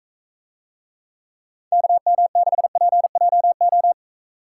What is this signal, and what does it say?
Morse code, heard as a keyed tone, about 44 wpm, 700 Hz. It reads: KM6PJO